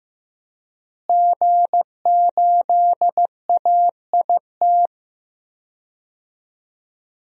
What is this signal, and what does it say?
Morse code, heard as a keyed tone, about 15 wpm, 700 Hz. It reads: G8AIT
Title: DXCC-100 Call Signs - 5 Characters - Repeated 3x - 15wpm